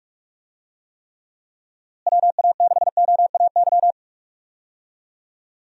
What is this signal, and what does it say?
Morse code, heard as a keyed tone, about 47 wpm, 700 Hz. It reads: WA6OAY
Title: USA Call Signs - 6 Characters - Repeated 2x - 45wpm